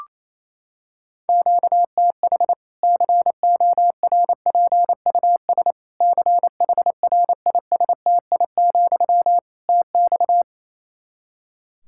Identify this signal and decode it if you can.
Morse code, heard as a keyed tone, about 28 wpm, 700 Hz. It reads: QTH CORPUS CHRISTI, TX
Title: K1USN SST QSO Practice 28wpm